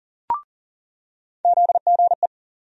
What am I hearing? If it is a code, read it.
Morse code, heard as a keyed tone, about 40 wpm, 700 Hz. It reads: ZGE
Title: USA Call Sign Postfixes - 3 Letters 40wpm